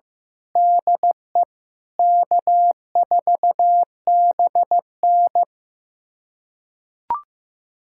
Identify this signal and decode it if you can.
Morse code, heard as a keyed tone, about 15 wpm, 700 Hz. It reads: DE K4BN
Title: USA Callsign DE Callsign - 4 Characters - 15wpm